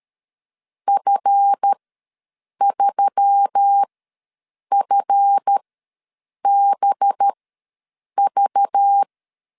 Morse code, heard as a keyed tone, about 13 wpm, spaced as for 7 wpm, 800 Hz. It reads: F3FBV
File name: LEZIONI / 7 GF9Z / 7 GF ws 12wpm 3140